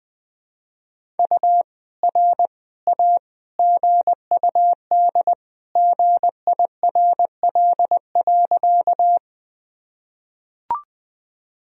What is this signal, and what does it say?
Morse code, heard as a keyed tone, about 20 wpm, 700 Hz. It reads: U R A GUD GIRL.
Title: Sentences from Top 400 Words and Abbreviations 20wpm